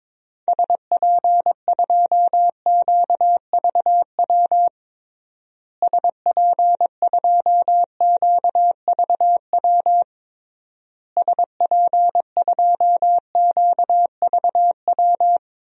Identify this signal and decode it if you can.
Morse code, heard as a keyed tone, about 22 wpm, 700 Hz. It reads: SP2QVW SP2QVW SP2QVW
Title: DXCC-100 Call Signs - 6 Characters - Repeated 3x - 22wpm